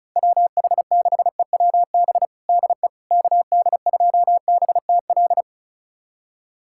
Morse code, heard as a keyed tone, about 36 wpm, 700 Hz. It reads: WH6EWB DE KD2BTL
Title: USA Callsign DE Callsign - 6 Characters - 35wpm